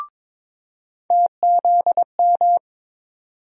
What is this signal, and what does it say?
Morse code, heard as a keyed tone, about 22 wpm, 700 Hz. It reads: TZM
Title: USA Call Sign Postfixes - 3 Letters 22wpm